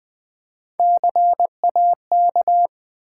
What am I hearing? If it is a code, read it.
Morse code, heard as a keyed tone, about 20 wpm, 700 Hz. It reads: CAK
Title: Build A Word - 1 Letter At A Time - 3 to 5-Letter Words - 20wpm